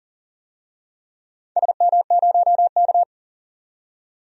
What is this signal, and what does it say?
Morse code, heard as a keyed tone, about 38 wpm, 700 Hz. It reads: SM0K
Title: DXCC-100 Call Signs - 4 Characters - 40wpm